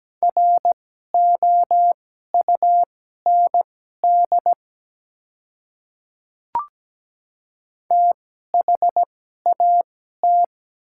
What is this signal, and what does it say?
Morse code, heard as a keyed tone, about 17 wpm, spaced as for 12 wpm, 700 Hz. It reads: ROUND THAT
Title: 2x Character Spacing - Top 200 Words - 17wpm